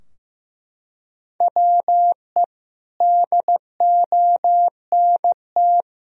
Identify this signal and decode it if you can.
Morse code, heard as a keyed tone, about 16 wpm, 700 Hz. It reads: WE DONT
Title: Sentences from Top 1000 Words - Review Part 4 - 15wpm